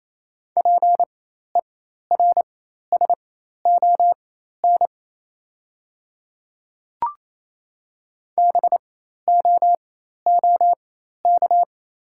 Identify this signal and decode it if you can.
Morse code, heard as a keyed tone, about 27 wpm, spaced as for 13 wpm, 700 Hz. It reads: PERSON BOOK
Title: 4x Character Spacing - Top 500 Words - 28wpm